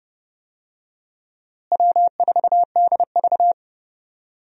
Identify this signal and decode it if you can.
Morse code, heard as a keyed tone, about 32 wpm, 700 Hz. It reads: W4DV
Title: USA Call Signs - 4 Characters - Repeated 2x - 30wpm